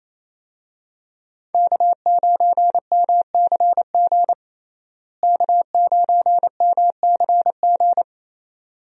Morse code, heard as a keyed tone, about 28 wpm, 700 Hz. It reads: K9MCG K9MCG
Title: USA Call Signs - 5 Characters - Repeated 2x - 28wpm